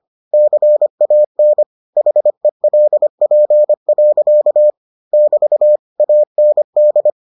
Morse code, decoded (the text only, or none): CAN HELP. <BT> AND